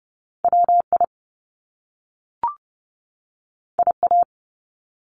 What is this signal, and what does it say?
Morse code, heard as a keyed tone, about 30 wpm, 700 Hz. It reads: WI IA
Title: US States 30wpm